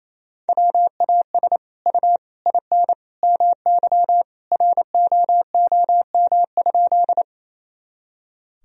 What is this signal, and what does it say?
Morse code, heard as a keyed tone, about 28 wpm, 700 Hz. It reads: WAS U IN MY ROOM?
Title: Sentences from Top 300 Words and Abbreviations 28wpm